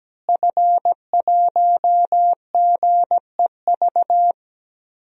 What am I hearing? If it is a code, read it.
Morse code, heard as a keyed tone, about 17 wpm, 700 Hz. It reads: F1GEV
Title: DXCC-100 Call Signs - 5 Characters - Repeated 3x - 17wpm